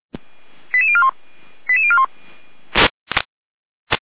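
Five-tone selective call (ZVEI), heard as quick runs of five tones, tones of about 70 ms. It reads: 88041 88041
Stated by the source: ZVEI five-tone selective call received via SDR, archived by the Feuerwehr Orlamünde fire brigade.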